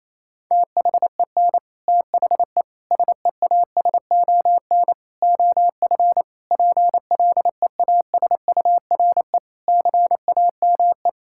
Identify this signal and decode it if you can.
Morse code, heard as a keyed tone, about 28 wpm, 700 Hz. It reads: THEN THE SEASON OF PLEASURE CAME
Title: Sentences from Top 1600 Words 28wpm